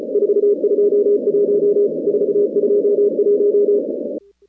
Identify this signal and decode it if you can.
Morse code, heard as a keyed tone, about 34 wpm, 410 Hz. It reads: 421 421
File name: contests / 2019-cw-wpx-kt5j-k5pi- -2330 0-41